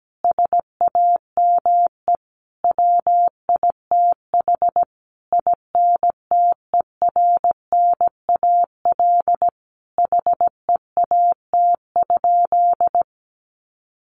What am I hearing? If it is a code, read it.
Morse code, heard as a keyed tone, about 17 wpm, 700 Hz. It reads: SAME WITH INTERNAL HEAT?
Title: Sentences from Top 1400 Words 17wpm